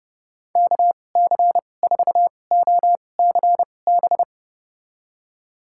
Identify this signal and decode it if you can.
Morse code, heard as a keyed tone, about 32 wpm, spaced as for 22 wpm, 700 Hz. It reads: KC4OCB
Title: USA Call Signs - 6 Characters - 2x Character Spacing - 30wpm